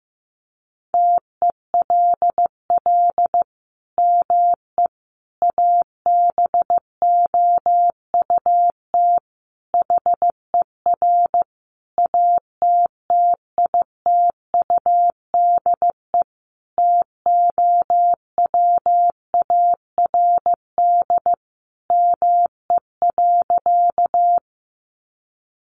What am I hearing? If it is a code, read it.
Morse code, heard as a keyed tone, about 15 wpm, 700 Hz. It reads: TELL ME ABOUT HER ATTITUDE TOWARD ME.
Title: Sentences from Top 1300 Words 15wpm